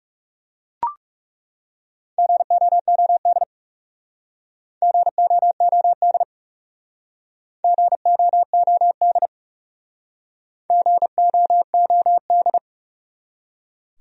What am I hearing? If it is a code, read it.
Morse code, heard as a keyed tone, about 40 wpm, 700 Hz. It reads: GOOD GOOD GOOD GOOD